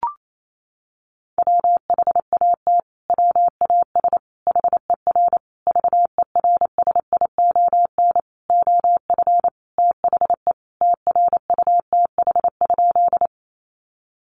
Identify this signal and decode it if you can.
Morse code, heard as a keyed tone, about 28 wpm, 700 Hz. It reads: WHAT WAS HER VERSION OF THE TRUTH?
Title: Sentences from Top 1400 Words 28wpm